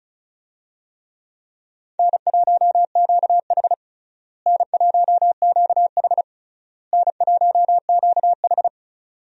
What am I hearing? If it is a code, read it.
Morse code, heard as a keyed tone, about 36 wpm, 700 Hz. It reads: N1QH N1QH N1QH
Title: USA Call Signs - 4 Characters - Repeated 3x - 35wpm